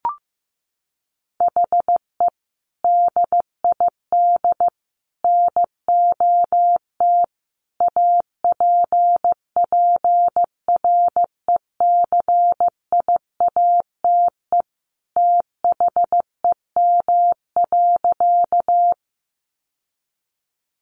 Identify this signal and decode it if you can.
Morse code, heard as a keyed tone, about 15 wpm, 700 Hz. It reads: HE DID NOT APPRECIATE THEM.
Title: Sentences from Top 1300 Words 15wpm